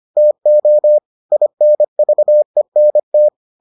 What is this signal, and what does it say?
Morse code, heard as a keyed tone, about 25 wpm, 600 Hz. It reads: TO INVENT